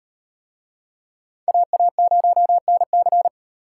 Morse code, heard as a keyed tone, about 38 wpm, 700 Hz. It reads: AA0NC